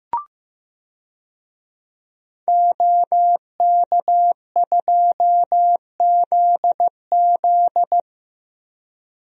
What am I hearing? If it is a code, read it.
Morse code, heard as a keyed tone, about 15 wpm, 700 Hz. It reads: OK2ZZ
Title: DXCC-100 Call Signs - 5 Characters - 15wpm